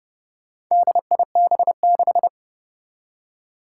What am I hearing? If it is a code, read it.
Morse code, heard as a keyed tone, about 32 wpm, 700 Hz. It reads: DIB6
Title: Four Challenging Letter-Number Combinations 30wpm